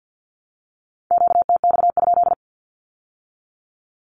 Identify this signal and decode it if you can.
Morse code, heard as a keyed tone, about 50 wpm, 700 Hz. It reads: QTX?
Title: Q-Codes with Spoken Meaning - 50wpm